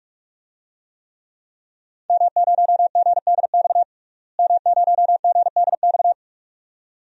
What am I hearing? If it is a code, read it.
Morse code, heard as a keyed tone, about 46 wpm, 700 Hz. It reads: M0GDX M0GDX
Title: DXCC-100 Call Signs - 5 Characters - Repeated 2x - 45wpm